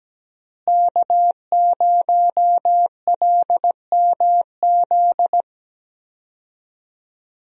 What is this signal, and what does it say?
Morse code, heard as a keyed tone, about 17 wpm, 700 Hz. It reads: K0LMZ
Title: USA Call Signs - 5 Characters - Repeated 3x - 17wpm